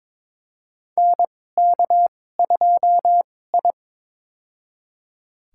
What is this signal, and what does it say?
Morse code, heard as a keyed tone, about 22 wpm, spaced as for 16 wpm, 700 Hz. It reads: NK2I
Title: USA Call Signs - 4 Characters - 2x Character Spacing - 22wpm